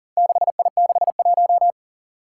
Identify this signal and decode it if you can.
Morse code, heard as a keyed tone, about 37 wpm, 700 Hz. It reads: 6I61
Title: Four Challenging Letter-Number Combinations 40wpm